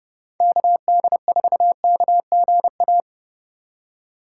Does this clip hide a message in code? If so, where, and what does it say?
Yes: Morse code from 0.4 to 3.0 s, heard as a keyed tone, about 33 wpm, 700 Hz. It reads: KD4KGA